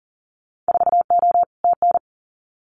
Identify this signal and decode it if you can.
Morse code, heard as a keyed tone, about 40 wpm, 700 Hz. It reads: 4O TN
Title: Winter Field Day Exchanges Encore 40wpm